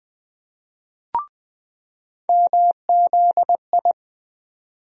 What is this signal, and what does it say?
Morse code, heard as a keyed tone, about 21 wpm, 700 Hz. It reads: MZI
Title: USA Call Sign Postfixes - 3 Letters 20wpm